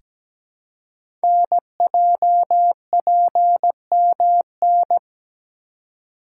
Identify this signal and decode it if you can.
Morse code, heard as a keyed tone, about 17 wpm, 700 Hz. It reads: NJPMN
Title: Code Groups - 5 Letters - 17wpm